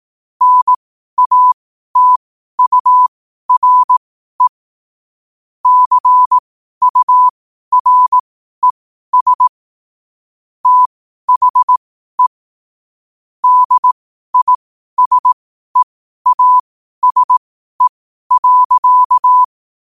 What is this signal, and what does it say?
Morse code, heard as a keyed tone, about 18 wpm, spaced as for 12 wpm, 1000 Hz. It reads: NATURE CURES THE DISEASE.
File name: qotdfiles / 12wpm